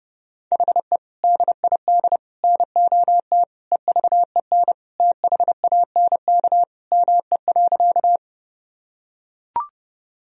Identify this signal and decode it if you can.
Morse code, heard as a keyed tone, about 30 wpm, 700 Hz. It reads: HE DID NOT EVEN THANK ME.
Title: Sentences from Top 1000 Words - Review Part 1 - 30wpm